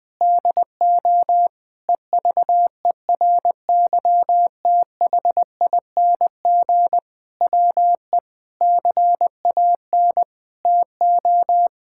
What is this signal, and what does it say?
Morse code, heard as a keyed tone, about 21 wpm, 700 Hz. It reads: DO EVERYTHING WE CAN TO
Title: Sentences with 8-Word Phrases - 20wpm